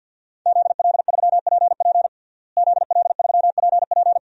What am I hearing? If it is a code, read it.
Morse code, heard as a keyed tone, about 52 wpm, 700 Hz. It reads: ZL3PP ZL3PP
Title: DXCC-100 Call Signs - 5 Characters - Repeated 2x - 50wpm